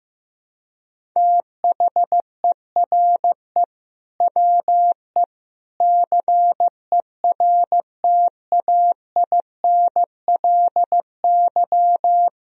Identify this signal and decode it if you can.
Morse code, heard as a keyed tone, about 15 wpm, 700 Hz. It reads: THERE WE CERTAINLY
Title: Sentences from Top 1000 Words - Review Part 1 - 15wpm